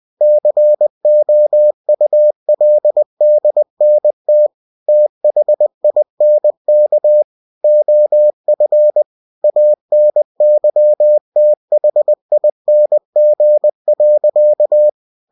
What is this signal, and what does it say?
Morse code, heard as a keyed tone, about 20 wpm, 600 Hz. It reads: COULDNT THINK OF ANYTHING.
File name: cw-books / winnie-the-pooh / winnie-20wpm-page-0050